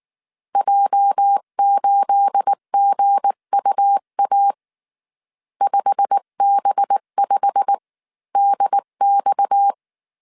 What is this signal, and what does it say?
Morse code, heard as a keyed tone, about 19 wpm, spaced as for 16 wpm, 800 Hz. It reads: J8GUA 5B5DX
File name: CW / DET / 1 9 1500C S B 18wpm 2551